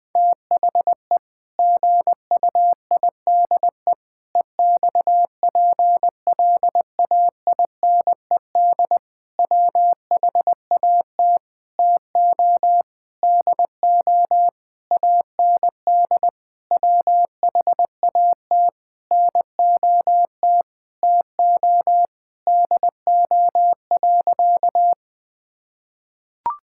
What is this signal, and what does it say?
Morse code, heard as a keyed tone, about 20 wpm, 700 Hz. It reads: THE GUIDE EXPLAINED WHAT TO DO AND WHAT NOT TO DO.
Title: Sentences with 8-Word Phrases - 20wpm